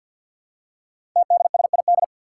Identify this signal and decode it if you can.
Morse code, heard as a keyed tone, about 49 wpm, 700 Hz. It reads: TDSID
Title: Five Challenging Letter-Number Combinations 50wpm